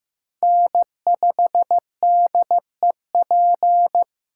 Morse code, heard as a keyed tone, about 18 wpm, 700 Hz. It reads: N5DEP
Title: Speed Racing - USA 1x3 Call Signs 15wpm